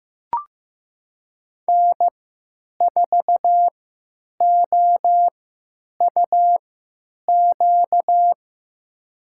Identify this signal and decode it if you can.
Morse code, heard as a keyed tone, about 15 wpm, spaced as for 9 wpm, 700 Hz. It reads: N4OUQ
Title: USA Call Signs - 5 Characters - 3x Character Spacing - 15wpm